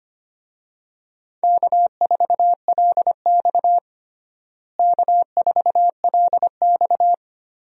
Morse code, heard as a keyed tone, about 25 wpm, 700 Hz. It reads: K4LX K4LX
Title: USA Call Signs - 4 Characters - Repeated 3x - 25wpm